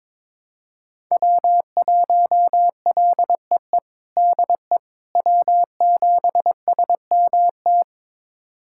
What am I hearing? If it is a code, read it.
Morse code, heard as a keyed tone, about 22 wpm, 700 Hz. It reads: W1LEE DE W7SMT
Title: USA Callsign DE Callsign - 5 Characters - 22wpm